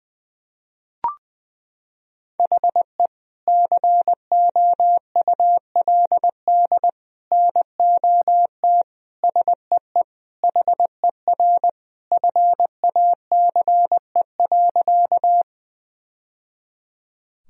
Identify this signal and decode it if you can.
Morse code, heard as a keyed tone, about 20 wpm, 700 Hz. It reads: HE COULD NOT SEE HER FACE.
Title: Sentences from Top 1000 Words - Review Part 2 - 20wpm